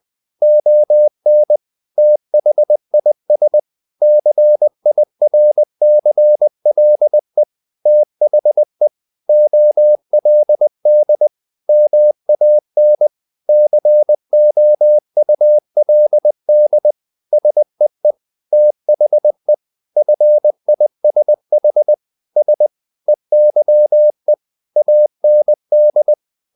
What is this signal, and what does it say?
Morse code, heard as a keyed tone, about 20 wpm, 600 Hz. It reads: ON THIS CIRCLE THE OLD MAN COULD SEE THE FISH S EYE AND
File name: the old man and the sea part 2 20wpm